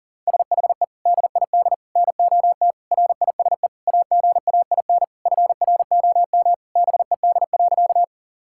Morse code, heard as a keyed tone, about 40 wpm, 700 Hz. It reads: SHE DID NOT RISE AGAIN FROM BED.
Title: Sentences from Top 1000 Words - Review Part 6 - 40wpm